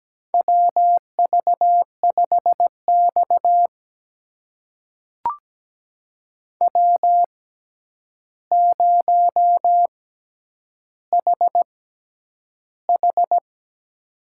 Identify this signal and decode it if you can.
Morse code, heard as a keyed tone, about 17 wpm, spaced as for 9 wpm, 700 Hz. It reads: WV5X W0HH
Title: USA Call Signs - 6x Character Spacing - 17wpm